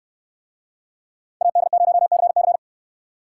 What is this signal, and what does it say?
Morse code, heard as a keyed tone, about 69 wpm, 700 Hz. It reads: AD0CC